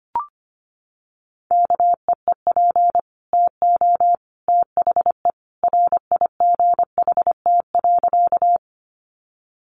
Morse code, heard as a keyed tone, about 25 wpm, 700 Hz. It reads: KEEP TO THE RIGHT.
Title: Sentences from Top 300 Words 25wpm